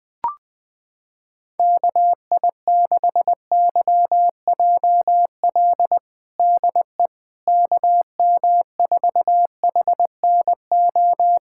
Morse code, heard as a keyed tone, about 21 wpm, 700 Hz. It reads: KI6YJL DE KM4HNO
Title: USA Callsign DE Callsign - 6 Characters - 20wpm